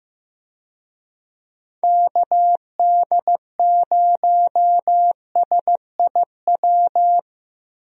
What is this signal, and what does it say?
Morse code, heard as a keyed tone, about 15 wpm, 700 Hz. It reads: KD0SIW
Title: USA Call Signs - 6 Characters - Repeated 2x - 15wpm